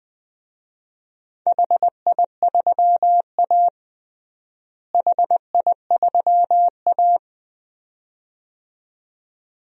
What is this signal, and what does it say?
Morse code, heard as a keyed tone, about 21 wpm, 700 Hz. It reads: HI3A HI3A
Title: DXCC-100 Call Signs - 4 Characters - Repeated 2x - 20wpm